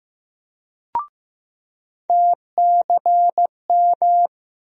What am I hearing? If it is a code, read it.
Morse code, heard as a keyed tone, about 15 wpm, 700 Hz. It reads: TCM
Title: USA Call Sign Postfixes - 3 Letters 15wpm